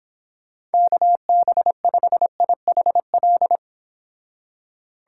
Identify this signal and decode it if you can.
Morse code, heard as a keyed tone, about 30 wpm, 700 Hz. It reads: KB5IHL